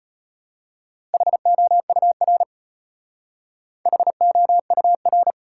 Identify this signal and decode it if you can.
Morse code, heard as a keyed tone, about 37 wpm, 700 Hz. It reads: HOUR HOUR